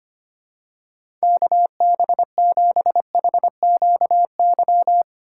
Morse code, heard as a keyed tone, about 25 wpm, 700 Hz. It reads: KB7HQY
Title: USA Call Signs - 6 Characters - 4x Character Spacing - 25wpm